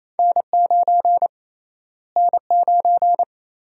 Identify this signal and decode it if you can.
Morse code, heard as a keyed tone, about 28 wpm, 700 Hz. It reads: N9 N9
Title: USA Call Sign Prefixes - Repeated 2x - 28wpm